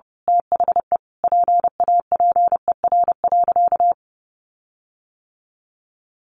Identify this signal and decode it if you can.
Morse code, heard as a keyed tone, about 30 wpm, 700 Hz. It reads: THE PAPER.
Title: Sentences from Top 1400 Words 30wpm